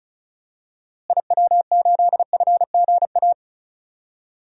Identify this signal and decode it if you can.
Morse code, heard as a keyed tone, about 36 wpm, 700 Hz. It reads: IW8FGA
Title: DXCC-100 Call Signs - 6 Characters - Repeated 3x - 35wpm